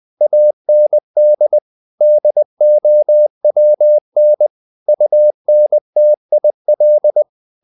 Morse code, heard as a keyed tone, about 20 wpm, 600 Hz. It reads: AND DOWN UNTIL
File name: the old man and the sea part 2 20wpm